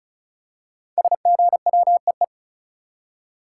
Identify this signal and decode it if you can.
Morse code, heard as a keyed tone, about 36 wpm, 700 Hz. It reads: SGWEE
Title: Code Groups - 5 Letters - 35wpm